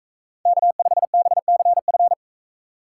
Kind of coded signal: Morse code

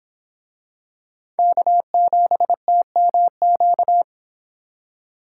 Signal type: Morse code

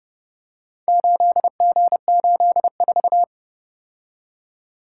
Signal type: Morse code

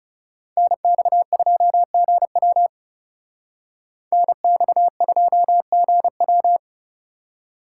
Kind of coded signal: Morse code

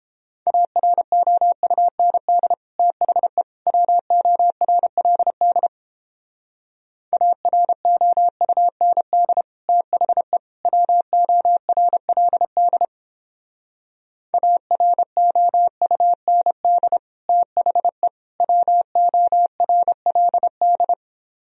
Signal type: Morse code

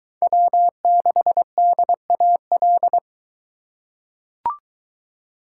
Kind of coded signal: Morse code

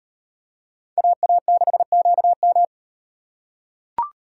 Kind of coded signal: Morse code

